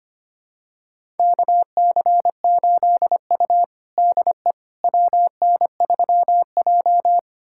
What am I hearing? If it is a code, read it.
Morse code, heard as a keyed tone, about 25 wpm, 700 Hz. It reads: KC8U DE WN3J
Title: USA Callsign DE Callsign - 4 Characters - 25wpm